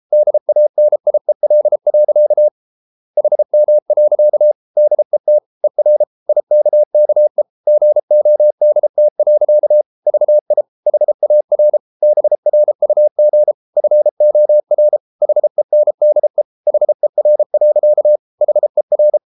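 Morse code, heard as a keyed tone, about 33 wpm, 600 Hz. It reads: DANIEL. HM. DET ER IKKE GODT. VI HAR BRUG FOR HENDE HER. HER
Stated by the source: Morse code at 33 wpm, QRQ.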